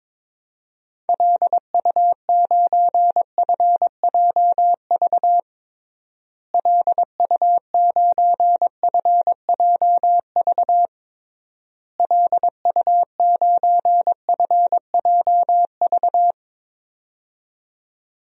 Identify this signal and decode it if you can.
Morse code, heard as a keyed tone, about 22 wpm, 700 Hz. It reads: LU9FJV LU9FJV LU9FJV
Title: DXCC-100 Call Signs - 6 Characters - Repeated 3x - 22wpm